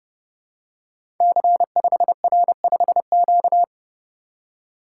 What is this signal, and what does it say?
Morse code, heard as a keyed tone, about 30 wpm, 700 Hz. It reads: C5R5Q
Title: Code Groups - 5 Letter:Numbers - 30wpm